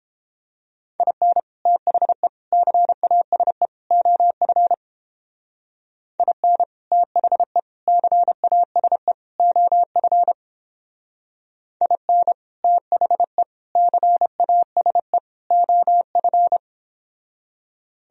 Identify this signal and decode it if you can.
Morse code, heard as a keyed tone, about 31 wpm, 700 Hz. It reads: IN THE CASE OF IN THE CASE OF IN THE CASE OF